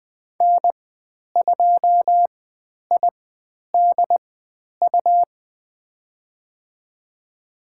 Morse code, heard as a keyed tone, about 20 wpm, spaced as for 10 wpm, 700 Hz. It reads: N2IDU
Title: 102 Lesson 17g - Letter u - Call Signs 20wpm